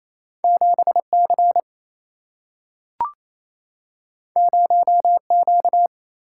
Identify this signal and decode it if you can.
Morse code, heard as a keyed tone, about 28 wpm, 700 Hz. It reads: 7 C 0 Q
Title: Single-Letter Single-Number 28wpm